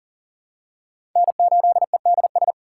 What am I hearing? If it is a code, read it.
Morse code, heard as a keyed tone, about 40 wpm, 700 Hz. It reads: N8EDS
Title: USA Call Signs - 5 Characters - 3x Character Spacing - 40wpm